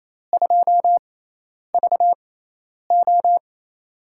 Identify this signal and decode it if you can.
Morse code, heard as a keyed tone, about 28 wpm, spaced as for 10 wpm, 700 Hz. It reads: 2VO
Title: USA Call Signs - 5 Characters - 6x Character Spacing - 28wpm